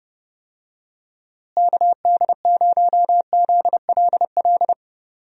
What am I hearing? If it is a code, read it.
Morse code, heard as a keyed tone, about 30 wpm, 700 Hz. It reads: KD0ZLL
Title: USA Call Signs - 6 Characters - Repeated 2x - 30wpm